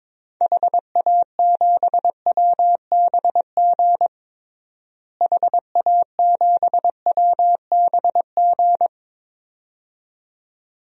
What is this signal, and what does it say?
Morse code, heard as a keyed tone, about 22 wpm, 700 Hz. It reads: HA7WBG HA7WBG
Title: DXCC-100 Call Signs - 6 Characters - Repeated 2x - 22wpm